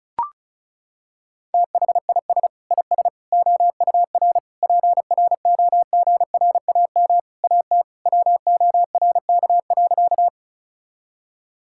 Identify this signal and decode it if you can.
Morse code, heard as a keyed tone, about 36 wpm, 700 Hz. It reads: THIS IS OUR PROGRAM AT WORK.
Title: Sentences from Top 1000 Words - Review Part 5 - 35wpm